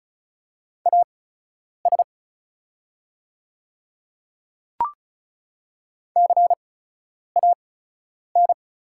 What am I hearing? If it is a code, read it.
Morse code, heard as a keyed tone, about 33 wpm, spaced as for 9 wpm, 690 Hz. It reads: AS CAN
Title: 8x Character Spacing - Top 100 Words - 35wpm